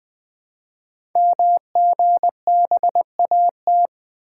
Morse code, heard as a keyed tone, about 20 wpm, 700 Hz. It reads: MGBAT
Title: Code Groups - 5 Letter:Numbers - 20wpm